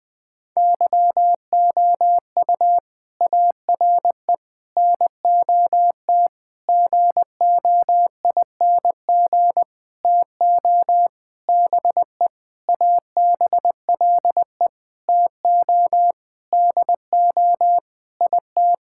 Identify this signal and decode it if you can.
Morse code, heard as a keyed tone, about 20 wpm, 700 Hz. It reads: YOU ARE NOT GOING TO BE ABLE TO DO IT
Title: Sentences with 7-Word Phrases - 20wpm